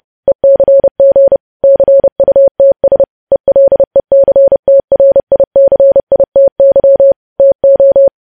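Morse code, heard as a keyed tone, about 30 wpm, 550 Hz. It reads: ECG CUTS ELECTRICITY TO
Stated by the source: Morse news headlines at 30 wpm.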